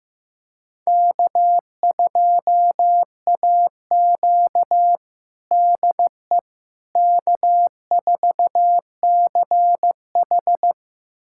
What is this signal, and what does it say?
Morse code, heard as a keyed tone, about 15 wpm, 700 Hz. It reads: K2AQ DE K4CH
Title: USA Callsign DE Callsign - 4 Characters - 15wpm